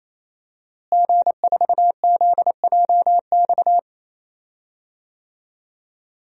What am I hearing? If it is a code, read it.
Morse code, heard as a keyed tone, about 27 wpm, 700 Hz. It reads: G4ZJX
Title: DXCC-100 Call Signs - 5 Characters - Repeated 3x - 28wpm